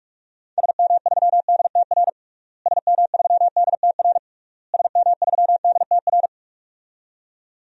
Morse code, heard as a keyed tone, about 46 wpm, 700 Hz. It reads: SM3DTR SM3DTR SM3DTR
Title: DXCC-100 Call Signs - 6 Characters - Repeated 3x - 45wpm